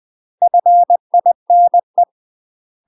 Morse code, heard as a keyed tone, about 20 wpm, 700 Hz. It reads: FINE
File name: UK MASTER QSOs TRAINING / 20WPM / UK MASTER QSOs 0025